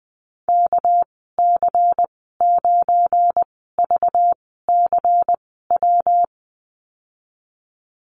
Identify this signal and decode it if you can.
Morse code, heard as a keyed tone, about 20 wpm, spaced as for 15 wpm, 700 Hz. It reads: KC9VCW
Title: 238 Lesson 38g - Letter q - Call Signs 20wpm